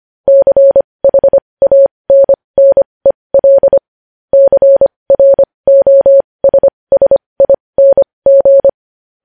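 Morse code, heard as a keyed tone, about 25 wpm, spaced as for 20 wpm, 550 Hz. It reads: CHANNEL CROSSING
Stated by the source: Morse code news headlines practice at 20 wpm.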